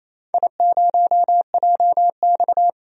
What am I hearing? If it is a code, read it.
Morse code, heard as a keyed tone, about 28 wpm, 700 Hz. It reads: I0JX
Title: DXCC-100 Call Signs - 4 Characters - 28wpm